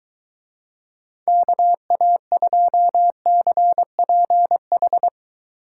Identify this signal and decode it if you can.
Morse code, heard as a keyed tone, about 23 wpm, 700 Hz. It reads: KA2CPH